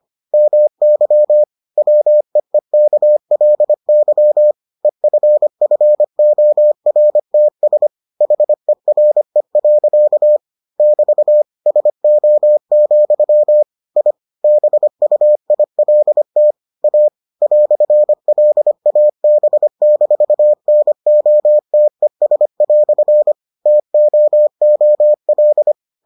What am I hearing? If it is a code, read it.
Morse code, heard as a keyed tone, about 25 wpm, 600 Hz. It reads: MY WEEKLY EFFORTS HERE. <BT> SO, I BUILT A "LAB-NOTES" TOOL